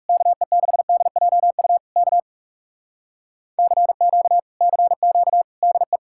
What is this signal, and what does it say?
Morse code, heard as a keyed tone, about 37 wpm, 700 Hz. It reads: KE6DJU K CQ CQ DE
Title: Speed-Racing Common QSO Phrases - 30wpm